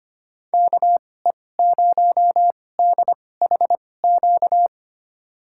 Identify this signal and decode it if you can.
Morse code, heard as a keyed tone, about 25 wpm, spaced as for 18 wpm, 700 Hz. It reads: KE0DHQ
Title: USA Call Signs - 2x Character Spacing - 25wpm